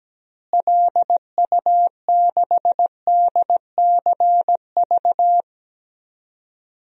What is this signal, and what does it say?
Morse code, heard as a keyed tone, about 17 wpm, 700 Hz. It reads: LU6DCV
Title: DXCC-100 Call Signs - 6 Characters - Repeated 2x - 17wpm